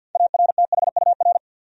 Morse code, heard as a keyed tone, about 52 wpm, 700 Hz. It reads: ARTHUR
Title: USA Common Names - 50wpm